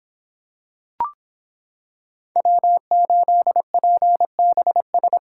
Morse code, heard as a keyed tone, about 26 wpm, 700 Hz. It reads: W8PBS